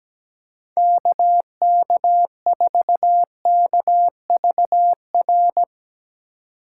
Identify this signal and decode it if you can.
Morse code, heard as a keyed tone, about 17 wpm, 700 Hz. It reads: KK4KVR
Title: Speed Racing - USA Call Signs 17wpm